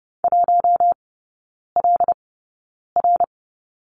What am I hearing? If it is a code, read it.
Morse code, heard as a keyed tone, about 30 wpm, spaced as for 9 wpm, 700 Hz. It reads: 1LR
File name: Lesson 18e - Letter m - Three Characters 30wpm